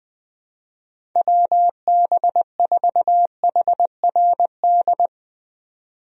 Morse code, heard as a keyed tone, about 21 wpm, 700 Hz. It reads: WB4HRD
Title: USA Call Signs - 6 Characters - 20wpm